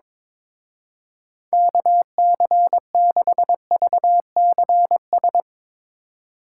Morse code, heard as a keyed tone, about 22 wpm, 700 Hz. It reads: KC6VCS
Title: Speed Racing - USA Call Signs 22wpm